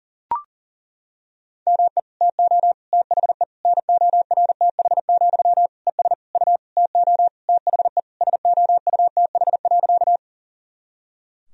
Morse code, heard as a keyed tone, about 40 wpm, 700 Hz. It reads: ME TO THE NORTH, ES U TO THE SOUTH.
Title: Sentences from Top 400 Words and Abbreviations 40wpm